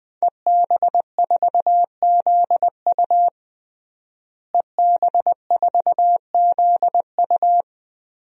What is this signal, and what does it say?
Morse code, heard as a keyed tone, about 20 wpm, 700 Hz. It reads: EB4ZU EB4ZU
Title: DXCC-100 Call Signs - 5 Characters - Repeated 2x - 20wpm